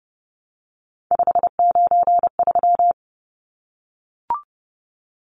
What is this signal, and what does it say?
Morse code, heard as a keyed tone, about 30 wpm, 700 Hz. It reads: 593
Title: Numbers - 3-Digit 30wpm